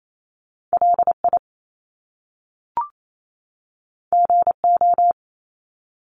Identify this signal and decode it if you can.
Morse code, heard as a keyed tone, about 28 wpm, 700 Hz. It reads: LI GO
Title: Top Two Letter Combinations in Words 28wpm